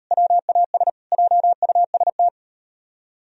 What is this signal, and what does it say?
Morse code, heard as a keyed tone, about 34 wpm, 700 Hz. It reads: WAS JUST